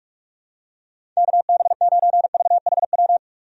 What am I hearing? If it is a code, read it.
Morse code, heard as a keyed tone, about 41 wpm, 700 Hz. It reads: KB9VHW
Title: USA Call Signs - 6 Characters - Repeated 2x - 45wpm